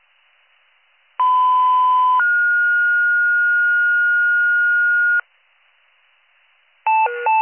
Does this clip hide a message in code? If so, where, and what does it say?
Yes: two-tone page from 1.2 to 5.2 s, heard as a one-second tone, then a three-second tone, about 1010 then 1430 Hz.